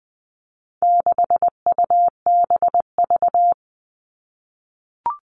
Morse code, heard as a keyed tone, about 20 wpm, 700 Hz. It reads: 6UBV
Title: All Cut Numbers - 4-Digit 20wpm